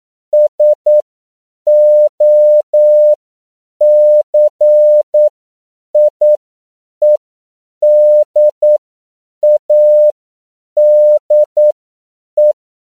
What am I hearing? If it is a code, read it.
Morse code, heard as a keyed tone, about 9 wpm, spaced as for 7 wpm, 600 Hz. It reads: SOCIEDADE
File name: CW / 10PPM